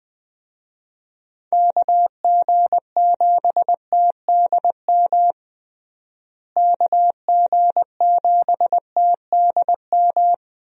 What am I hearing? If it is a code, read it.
Morse code, heard as a keyed tone, about 20 wpm, 700 Hz. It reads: KG7TDM KG7TDM
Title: USA Call Signs - 6 Characters - Repeated 2x - 20wpm